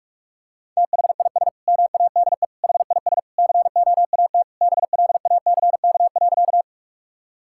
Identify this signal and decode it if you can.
Morse code, heard as a keyed tone, about 41 wpm, 700 Hz. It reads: THIS MADE HIS COAT BLACK.
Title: Sentences from Top 1600 Words 45wpm